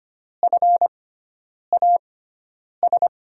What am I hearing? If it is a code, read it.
Morse code, heard as a keyed tone, about 25 wpm, spaced as for 9 wpm, 700 Hz. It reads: FAS
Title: USA Call Signs - 6 Characters - 6x Character Spacing - 25wpm